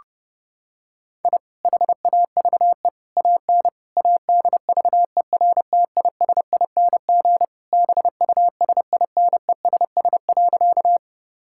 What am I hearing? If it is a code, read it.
Morse code, heard as a keyed tone, about 32 wpm, 700 Hz. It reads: I HAVE AN ADVERTISING BUSINESS.
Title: Sentences from Top 1000 Words - Review Part 6 - 30wpm